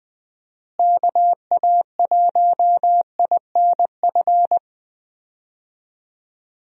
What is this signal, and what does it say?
Morse code, heard as a keyed tone, about 20 wpm, 700 Hz. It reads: KA1INF
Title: USA Call Signs - 6 Characters - Repeated 3x - 20wpm